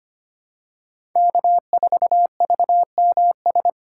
Morse code, heard as a keyed tone, about 25 wpm, 700 Hz. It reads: K4VMS
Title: USA Call Signs - 3x Character Spacing - 25wpm